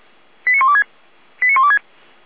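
Five-tone selective call (ZVEI), heard as quick runs of five tones, tones of about 70 ms. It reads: 89126 89126
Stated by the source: ZVEI five-tone call received by SDR, recorded by feuerwehr-orlamuende.org (fire brigade Orlamünde).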